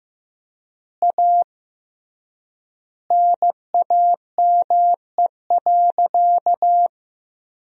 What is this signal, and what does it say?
Morse code, heard as a keyed tone, about 15 wpm, 700 Hz. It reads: A NAME.
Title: 3x Word-Spacing - Sentences from Top 500 Words - 15wpm